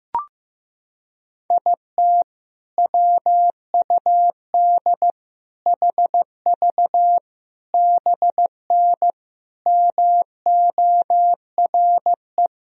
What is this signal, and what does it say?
Morse code, heard as a keyed tone, about 16 wpm, 700 Hz. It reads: IT WUD HV BN MORE
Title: Sentences from Top 400 Words and Abbreviations 15wpm